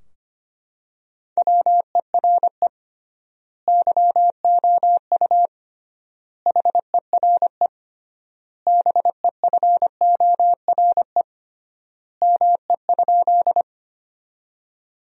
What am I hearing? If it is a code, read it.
Morse code, heard as a keyed tone, about 25 wpm, 700 Hz. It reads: WERE YOU HERE BEFORE ME?
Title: 3x Word-Spacing - Sentences from Top 200 Words - 25wpm